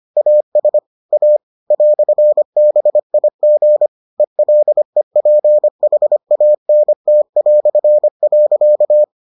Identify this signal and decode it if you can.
Morse code, heard as a keyed tone, about 25 wpm, 600 Hz. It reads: AS A "BIG ELEPHANT".